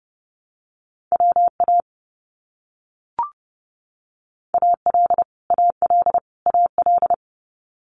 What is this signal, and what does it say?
Morse code, heard as a keyed tone, about 30 wpm, 700 Hz. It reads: WA AL AL AL
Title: US States Repeated 3x 30wpm